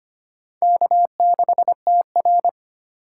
Morse code, heard as a keyed tone, about 25 wpm, 700 Hz. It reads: K6TR
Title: USA Call Signs - 4 Characters - 2x Character Spacing - 25wpm